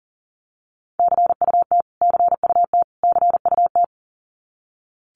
Morse code, heard as a keyed tone, about 37 wpm, 700 Hz. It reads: CUT CUT CUT
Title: Top 500 Words Repeated 40wpm